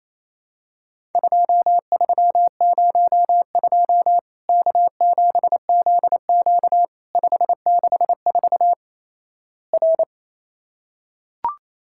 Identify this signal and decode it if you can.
Morse code, heard as a keyed tone, about 28 wpm, 700 Hz. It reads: 2302 K7ZQ 564 R
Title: Worked All Europe DX Contest - QTC Exchange - 28wpm